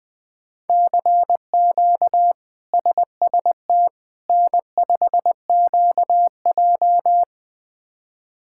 Morse code, heard as a keyed tone, about 20 wpm, 700 Hz. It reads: CQ SST N5QJ
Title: K1USN SST QSO Practice - 20wpm